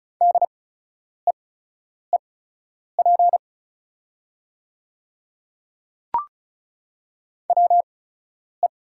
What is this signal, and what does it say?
Morse code, heard as a keyed tone, about 34 wpm, spaced as for 10 wpm, 690 Hz. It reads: DEEP WE
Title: 8x Character Spacing - Top 500 Words - 35wpm